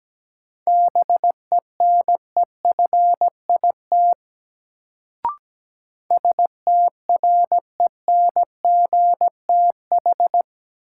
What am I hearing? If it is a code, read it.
Morse code, heard as a keyed tone, about 17 wpm, 700 Hz. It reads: BENEFIT STRENGTH
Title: Sentences from Top 1300 Words - New Word Drill Alt - 17wpm